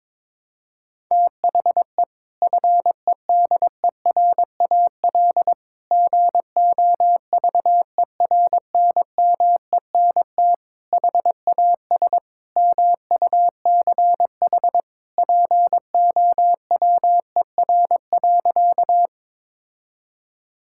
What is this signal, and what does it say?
Morse code, heard as a keyed tone, about 22 wpm, 700 Hz. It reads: THE FEDERAL GOVERNMENT HAS MUCH POWER.
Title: Sentences from Top 1000 Words - Review Part 5 - 22wpm